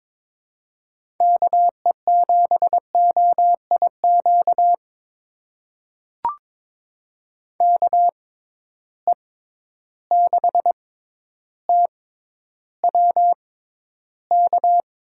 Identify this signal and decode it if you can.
Morse code, heard as a keyed tone, about 22 wpm, spaced as for 11 wpm, 700 Hz. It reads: KE7OIQ KE6TWK
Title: USA Call Signs - 6 Characters - 6x Character Spacing - 22wpm